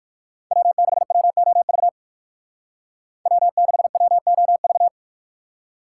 Morse code, heard as a keyed tone, about 55 wpm, 700 Hz. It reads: W6WOV W6WOV
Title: Speed Racing - USA 1x3 Call Signs 45wpm